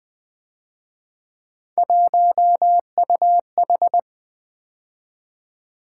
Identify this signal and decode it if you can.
Morse code, heard as a keyed tone, about 24 wpm, 700 Hz. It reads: 1UH
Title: Speed-Racing - Three Letter-Number Combinations - 20wpm